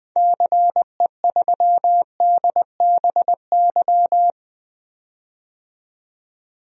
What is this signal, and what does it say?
Morse code, heard as a keyed tone, about 20 wpm, 700 Hz. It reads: CE3DBY
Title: DXCC-100 Call Signs - 6 Characters - 20wpm